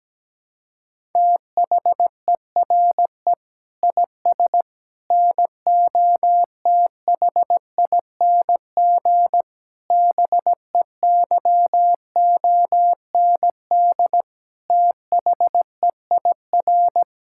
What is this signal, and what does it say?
Morse code, heard as a keyed tone, about 17 wpm, 700 Hz. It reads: THERE IS NOTHING BEYOND THEIR
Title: Sentences from Top 1000 Words - Review Part 6 - 17wpm